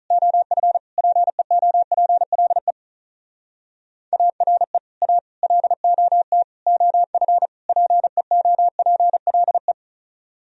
Morse code, heard as a keyed tone, about 46 wpm, 700 Hz. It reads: OF PEOPLE ARE A LOT OF PEOPLE